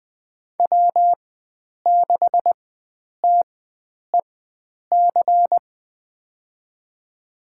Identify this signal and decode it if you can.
Morse code, heard as a keyed tone, about 20 wpm, spaced as for 9 wpm, 700 Hz. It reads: W6TEC
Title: USA Call Signs - 5 Characters - 4x Character Spacing - 20wpm